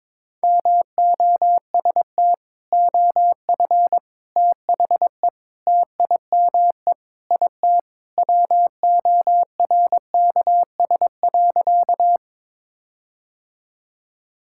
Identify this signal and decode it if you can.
Morse code, heard as a keyed tone, about 22 wpm, 700 Hz. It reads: MOST OF THE TIME IT WORKS.
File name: practice / 22wpm / Sentences with 4-Word Phrases - 22wpm